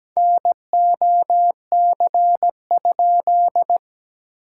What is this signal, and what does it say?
Morse code, heard as a keyed tone, about 17 wpm, 700 Hz. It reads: NOC?
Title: SOTA QSO Practice 17wpm